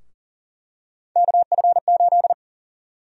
Morse code, heard as a keyed tone, about 40 wpm, 700 Hz. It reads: KF8
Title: USA Call Sign Prefixes - 40wpm